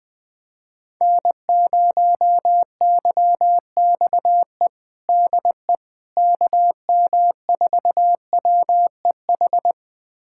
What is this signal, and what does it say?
Morse code, heard as a keyed tone, about 21 wpm, 700 Hz. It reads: N0YXE DE KM4WEH
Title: USA Callsign DE Callsign 20wpm